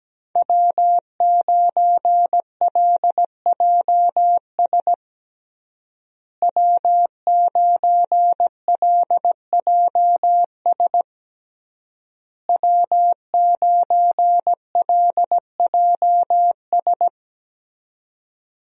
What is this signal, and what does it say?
Morse code, heard as a keyed tone, about 17 wpm, 700 Hz. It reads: W9LJS W9LJS W9LJS
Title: USA Call Signs - 5 Characters - Repeated 3x - 17wpm